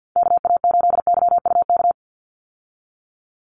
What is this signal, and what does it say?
Morse code, heard as a keyed tone, about 50 wpm, 700 Hz. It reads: KA8YUK
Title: USA Call Signs 50wpm